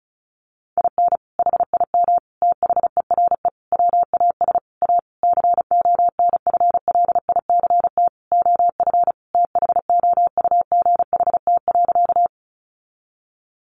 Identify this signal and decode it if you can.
Morse code, heard as a keyed tone, about 34 wpm, 700 Hz. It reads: IN HIM THERE WAS A CONFLICT OF THOUGHT.
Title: Sentences from Top 1300 Words 35wpm